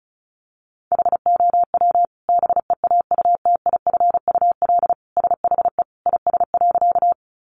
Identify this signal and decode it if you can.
Morse code, heard as a keyed tone, about 34 wpm, 700 Hz. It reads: HOW BEAUTIFUL SHE IS.
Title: Sentences from Top 900 Words 35wpm